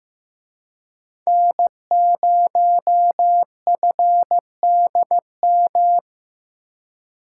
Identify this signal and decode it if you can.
Morse code, heard as a keyed tone, about 18 wpm, 700 Hz. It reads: N0FDM
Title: Speed Racing - USA 1x3 Call Signs 15wpm